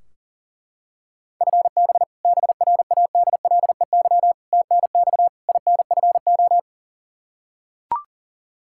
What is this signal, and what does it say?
Morse code, heard as a keyed tone, about 40 wpm, 700 Hz. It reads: FB BRADLEY TNX INFO
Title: Practice QSO - Exchange 3 - More Info - 40wpm